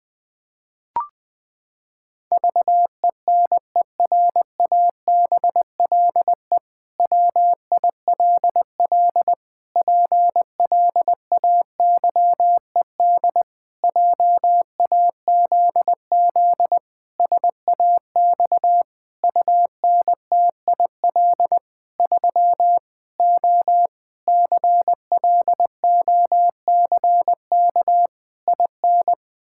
Morse code, heard as a keyed tone, about 21 wpm, 700 Hz. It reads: VENERABLE WILL PLAYED JAZZ SAX UNTIL 3 O CLOCK IN
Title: Pangram Challenge v2 - 20wpm